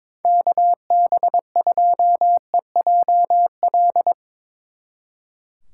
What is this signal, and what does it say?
Morse code, heard as a keyed tone, about 22 wpm, 700 Hz. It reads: KB2EJL
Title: Speed Racing - USA 2x3 Call Signs 22wpm